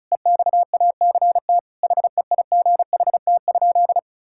Morse code, heard as a keyed tone, about 34 wpm, 700 Hz. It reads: EXACT HEIGHT?
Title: Sentences from Top 1600 Words 35wpm